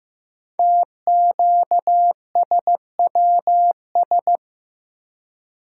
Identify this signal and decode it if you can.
Morse code, heard as a keyed tone, about 15 wpm, 700 Hz. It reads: TQSWS
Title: Code Groups - 5 Letters - 15wpm